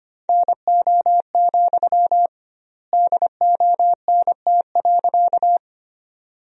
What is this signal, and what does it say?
Morse code, heard as a keyed tone, about 25 wpm, 700 Hz. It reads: NO, DONT.
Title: 2x Word-Spacing - Sentences from Top 300 Words - 25wpm